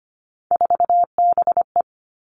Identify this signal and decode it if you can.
Morse code, heard as a keyed tone, about 25 wpm, 700 Hz. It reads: 4BE
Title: All Cut Numbers - 3-Digit 25wpm